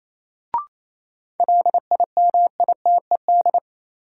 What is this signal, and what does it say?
Morse code, heard as a keyed tone, about 28 wpm, 700 Hz. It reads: LIMITED
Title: Sentences from Top 900 Words - New Word Drill - 28wpm